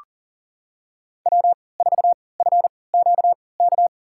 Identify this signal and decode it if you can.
Morse code, heard as a keyed tone, about 37 wpm, spaced as for 22 wpm, 700 Hz. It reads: W4FQK
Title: USA Call Signs - 5 Characters - 3x Character Spacing - 40wpm